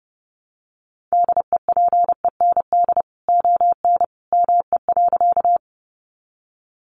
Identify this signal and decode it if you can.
Morse code, heard as a keyed tone, about 30 wpm, 700 Hz. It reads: DEPEND ON ME.
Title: Sentences from Top 1300 Words 30wpm